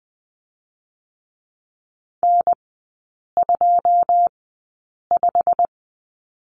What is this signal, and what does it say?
Morse code, heard as a keyed tone, about 21 wpm, spaced as for 8 wpm, 690 Hz. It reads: N25
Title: 087 Lesson 15e - Number 5 - Three Characters 20wpm